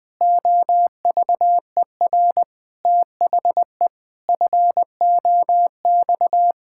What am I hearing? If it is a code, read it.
Morse code, heard as a keyed tone, about 20 wpm, 700 Hz. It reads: OVER THE FOX
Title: Pangram Challenge v2 - 20wpm